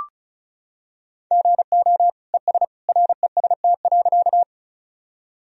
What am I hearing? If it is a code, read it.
Morse code, heard as a keyed tone, about 34 wpm, 700 Hz. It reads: GO ES REST.
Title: Sentences from Top 500 Words and Abbreviations 35wpm